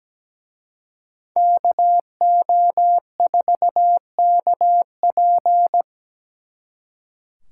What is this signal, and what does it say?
Morse code, heard as a keyed tone, about 20 wpm, 700 Hz. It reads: KO4KP
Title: Speed Racing - USA 2x2 Call Signs 17wpm